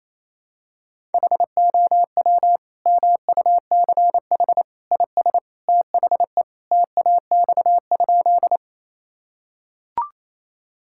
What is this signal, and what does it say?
Morse code, heard as a keyed tone, about 28 wpm, 700 Hz. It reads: HOW MUCH IS THE TAX?
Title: Sentences from Top 1000 Words - Review Part 5 - 28wpm